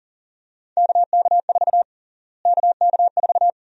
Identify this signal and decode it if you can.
Morse code, heard as a keyed tone, about 40 wpm, 700 Hz. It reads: KK4 KK4
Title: USA Call Sign Prefixes - Repeated 2x - 40wpm